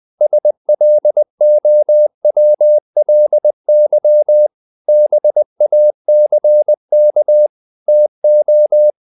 Morse code, heard as a keyed tone, about 20 wpm, 600 Hz. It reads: SLOWLY BACK TO
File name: the old man and the sea part 2 20wpm